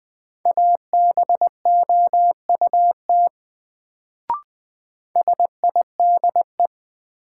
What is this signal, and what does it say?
Morse code, heard as a keyed tone, about 20 wpm, 700 Hz. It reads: ABOUT SIDE
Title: Sets of 2 Words - New Word Drill - 20wpm